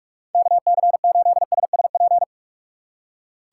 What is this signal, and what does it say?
Morse code, heard as a keyed tone, about 46 wpm, 700 Hz. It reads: KC8SSP
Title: USA Call Signs - 6 Characters - 6x Character Spacing - 45wpm